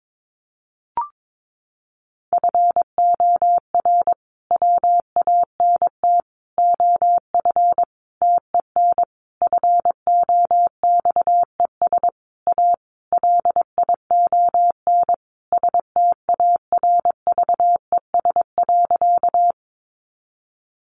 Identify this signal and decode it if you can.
Morse code, heard as a keyed tone, about 22 wpm, 700 Hz. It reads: FOR WANT OF TEN FOXES A LION STARVES.
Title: A Sound Of Thunder 22wpm